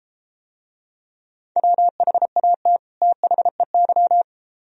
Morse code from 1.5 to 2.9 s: WHAT

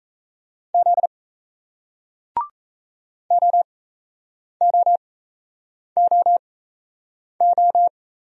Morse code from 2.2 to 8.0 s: OOOO

